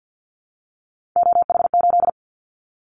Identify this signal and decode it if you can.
Morse code, heard as a keyed tone, about 50 wpm, 700 Hz. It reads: OH8
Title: Callsign Prefixes for Top 50 DXCC - 50wpm